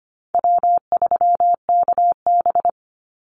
Morse code, heard as a keyed tone, about 25 wpm, 700 Hz. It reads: W3KB
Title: Easy Call Signs - 25wpm